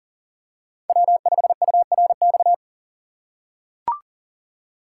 Morse code, heard as a keyed tone, about 40 wpm, 700 Hz. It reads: W5URX
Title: USA Call Signs - 2x Character Spacing - 40wpm